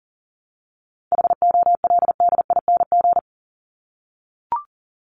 Morse code, heard as a keyed tone, about 40 wpm, 700 Hz. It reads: HOLDING
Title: ING Suffix - Top 100 Words with Base Words 40wpm